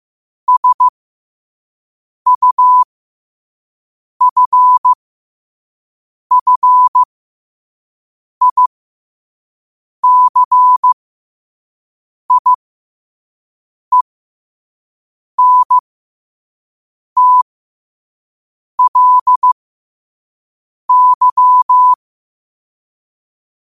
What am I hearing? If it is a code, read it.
Morse code, heard as a keyed tone, about 15 wpm, spaced as for 5 wpm, 1000 Hz. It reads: SUFFICIENTLY
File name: qotdfiles / 5wpm